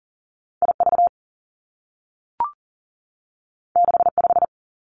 Morse code, heard as a keyed tone, about 42 wpm, 700 Hz. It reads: I V 6 5
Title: Two Challenging Letter-Number Combinations 40wpm